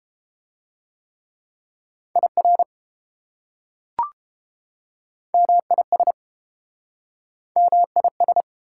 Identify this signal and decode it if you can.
Morse code, heard as a keyed tone, about 28 wpm, 700 Hz. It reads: IR MIS MIS